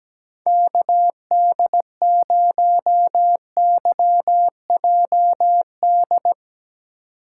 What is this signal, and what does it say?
Morse code, heard as a keyed tone, about 17 wpm, 700 Hz. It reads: KD0YJD
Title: USA Call Signs - 6 Characters - 2x Character Spacing - 17wpm